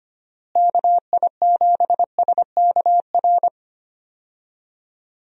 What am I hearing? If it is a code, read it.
Morse code, heard as a keyed tone, about 25 wpm, 700 Hz. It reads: KI7SKR
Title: USA Call Signs - 6 Characters - Repeated 2x - 25wpm